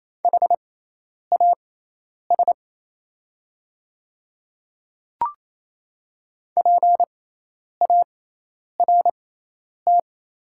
Morse code, heard as a keyed tone, about 28 wpm, spaced as for 10 wpm, 700 Hz. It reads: HAS PART
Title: 6x Character Spacing - Top 200 Words - 28wpm